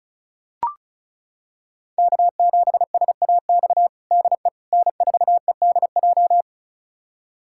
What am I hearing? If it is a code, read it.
Morse code, heard as a keyed tone, about 36 wpm, 700 Hz. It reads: K7SAX DE N4EDJ
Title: USA Callsign DE Callsign - 5 Characters - 35wpm